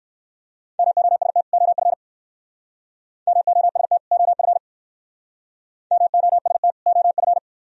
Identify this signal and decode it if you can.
Morse code, heard as a keyed tone, about 57 wpm, 700 Hz. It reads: MOST OF MOST OF MOST OF